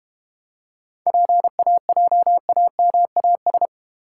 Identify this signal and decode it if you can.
Morse code, heard as a keyed tone, about 31 wpm, 700 Hz. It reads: PAJAMAS
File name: practice / 25wpm / Difficult Words - Speed-Racing with Hints - 25wpm